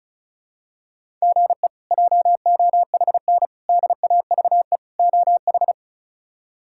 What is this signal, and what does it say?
Morse code, heard as a keyed tone, about 34 wpm, 700 Hz. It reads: GE JOHN DAVE OH
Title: K1USN SST QSO Practice - 35wpm